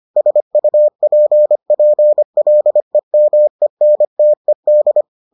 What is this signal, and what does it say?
Morse code, heard as a keyed tone, about 25 wpm, 600 Hz. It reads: SUPPLEMENTED